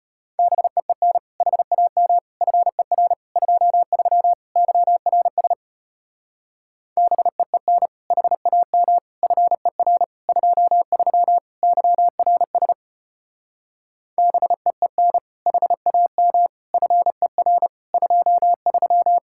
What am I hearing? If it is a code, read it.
Morse code, heard as a keyed tone, about 33 wpm, 700 Hz. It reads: BEEN HAM FER 23 YRS BEEN HAM FER 23 YRS BEEN HAM FER 23